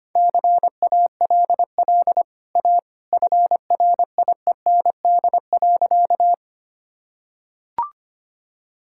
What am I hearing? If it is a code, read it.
Morse code, heard as a keyed tone, about 25 wpm, 700 Hz. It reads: CALL A FRIEND.
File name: Easy Sentences with Hints - 25wpm